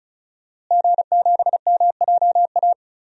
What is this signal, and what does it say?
Morse code, heard as a keyed tone, about 34 wpm, 700 Hz. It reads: G7MJA
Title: DXCC-100 Call Signs - 5 Characters - Repeated 2x - 35wpm